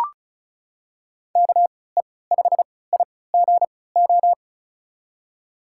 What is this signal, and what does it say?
Morse code, heard as a keyed tone, about 36 wpm, spaced as for 20 wpm, 700 Hz. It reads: KE5IGO
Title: USA Call Signs - 6 Characters - 3x Character Spacing - 35wpm